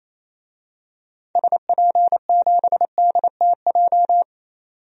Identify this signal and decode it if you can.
Morse code, heard as a keyed tone, about 28 wpm, 700 Hz. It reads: SP7DTJ
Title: DXCC-100 Call Signs - 6 Characters - 28wpm